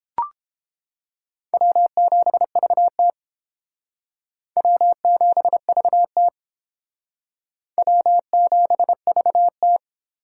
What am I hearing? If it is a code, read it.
Morse code, heard as a keyed tone, about 28 wpm, 700 Hz. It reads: W7VT W7VT W7VT